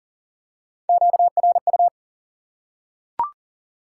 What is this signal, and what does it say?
Morse code, heard as a keyed tone, about 40 wpm, 700 Hz. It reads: QRU
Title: Top 90 QSO Elements 40wpm